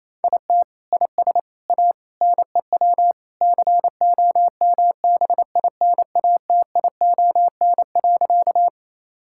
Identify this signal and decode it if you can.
Morse code, heard as a keyed tone, about 28 wpm, 700 Hz. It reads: IT IS A NEW COMBINATION.
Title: Sentences from Top 1000 Words - Review Part 6 - 28wpm